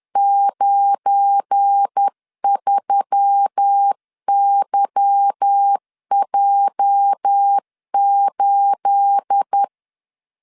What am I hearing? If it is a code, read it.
Morse code, heard as a keyed tone, about 10 wpm, 800 Hz. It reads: 93YJ8
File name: LEZIONI / 9 28 10wpm 2842